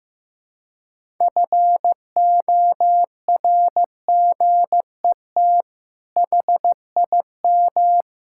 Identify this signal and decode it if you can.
Morse code, heard as a keyed tone, about 15 wpm, 700 Hz. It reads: FORGET HIM
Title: Sentences from Top 1600 Words 15wpm